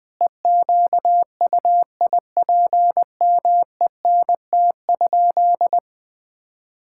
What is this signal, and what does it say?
Morse code, heard as a keyed tone, about 20 wpm, 700 Hz. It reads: EQUIPMENT?
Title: Sentences from Top 1000 Words - Review Part 5 - 20wpm